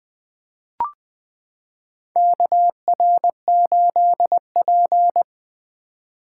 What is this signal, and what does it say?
Morse code, heard as a keyed tone, about 20 wpm, 700 Hz. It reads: KR8P
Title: 2023 POTA Hunter Call Signs - 20wpm